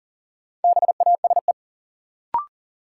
Morse code, heard as a keyed tone, about 38 wpm, 700 Hz. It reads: BASE
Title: Sets of 5 Words - New Word Drill - 40wpm